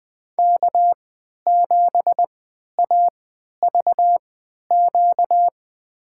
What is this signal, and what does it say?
Morse code, heard as a keyed tone, about 20 wpm, spaced as for 11 wpm, 700 Hz. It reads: K7AVQ
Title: USA Call Signs - 5 Characters - 3x Character Spacing - 20wpm